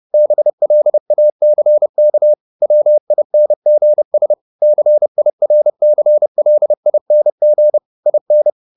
Morse code, heard as a keyed tone, about 30 wpm, 600 Hz. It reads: BLACK WINGS CIRCLING IN
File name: the old man and the sea / the old man and the sea 1 30wpm